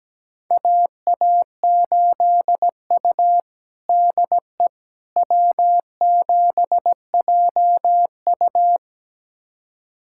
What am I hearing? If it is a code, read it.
Morse code, heard as a keyed tone, about 17 wpm, 700 Hz. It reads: AA8U DE W7JU
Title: USA Callsign DE Callsign 17wpm